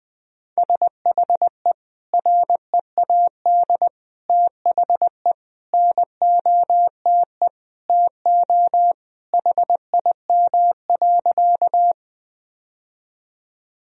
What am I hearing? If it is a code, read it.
Morse code, heard as a keyed tone, about 20 wpm, 700 Hz. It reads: SHE READ THE NOTE TO HIM.
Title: Sentences from Top 1000 Words - Review Part 2 - 20wpm